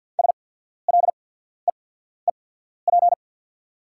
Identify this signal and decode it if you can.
Morse code, heard as a keyed tone, about 48 wpm, spaced as for 14 wpm, 690 Hz. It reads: SLEEP
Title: 8x Character Spacing - Top 500 Words - 50wpm